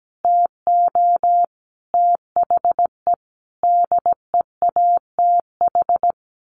Morse code, heard as a keyed tone, about 17 wpm, 700 Hz. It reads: TO THE DEATH
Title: Sentences from Top 1500 Words 17wpm